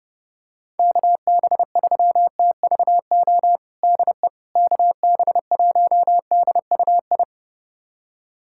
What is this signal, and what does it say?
Morse code, heard as a keyed tone, about 32 wpm, 700 Hz. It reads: KB3TVO DE KB1DUI
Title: USA Callsign DE Callsign - 6 Characters - 30wpm